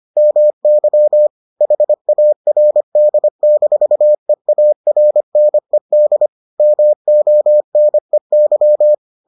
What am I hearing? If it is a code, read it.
Morse code, heard as a keyed tone, about 25 wpm, 600 Hz. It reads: MY HARD-EARNED MONEY